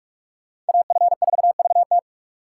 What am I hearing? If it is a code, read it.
Morse code, heard as a keyed tone, about 56 wpm, 690 Hz. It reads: AF4VT
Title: Speed Racing - USA 2x2 Call Signs 45wpm